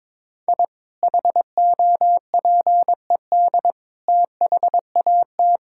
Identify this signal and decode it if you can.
Morse code, heard as a keyed tone, about 21 wpm, 700 Hz. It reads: I HOPED THAT
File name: practice / 22wpm / Sentences with 4-Word Phrases - 22wpm